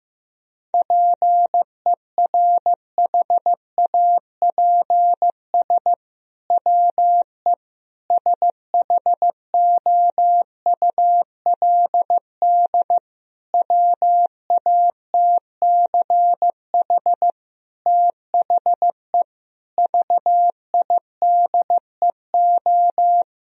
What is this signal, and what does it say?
Morse code, heard as a keyed tone, about 15 wpm, 700 Hz. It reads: PERHAPS WE SHOULD WATCH THE VIDEO
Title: Sentences from Top 1000 Words - Review Part 4 - 15wpm